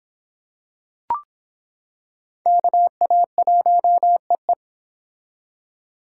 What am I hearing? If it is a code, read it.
Morse code, heard as a keyed tone, about 25 wpm, 700 Hz. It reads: KA1EE